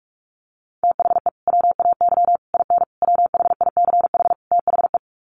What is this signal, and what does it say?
Morse code, heard as a keyed tone, about 44 wpm, 700 Hz. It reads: THE WAY IN WHICH THE
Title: Sets of 5 Words Encore2 45wpm